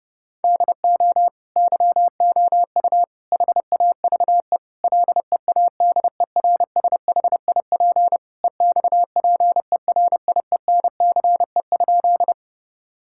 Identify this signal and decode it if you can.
Morse code, heard as a keyed tone, about 30 wpm, 700 Hz. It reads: DO YOU HAVE LEADERSHIP EXPERIENCE?
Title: Sentences from Top 1600 Words 30wpm